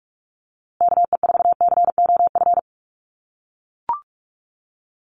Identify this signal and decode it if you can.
Morse code, heard as a keyed tone, about 44 wpm, 700 Hz. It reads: KE4COF
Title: USA Call Signs Repeated 3x 45wpm